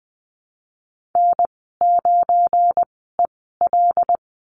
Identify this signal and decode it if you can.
Morse code, heard as a keyed tone, about 20 wpm, spaced as for 15 wpm, 700 Hz. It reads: N9EL
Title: 173 Lesson 28g - Number 9 - Call Signs 20wpm